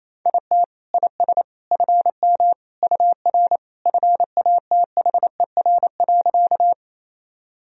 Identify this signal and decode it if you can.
Morse code, heard as a keyed tone, about 28 wpm, 700 Hz. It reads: IT IS FM UR FATHER.
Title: Sentences from Top 300 Words and Abbreviations 28wpm